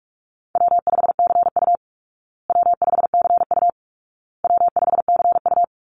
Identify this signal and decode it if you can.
Morse code, heard as a keyed tone, about 44 wpm, 700 Hz. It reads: W5CU W5CU W5CU
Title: Easy Call Signs Repeated 3x - 45wpm